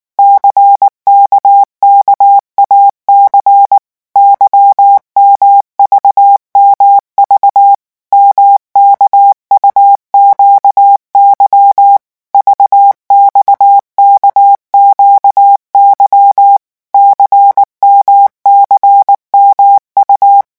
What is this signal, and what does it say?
Morse code, heard as a keyed tone, about 19 wpm, 800 Hz. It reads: CKKAC YMVMV MKUQY VXKQY CMCMU
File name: morse mp3 / revision2-vitesse 18 mots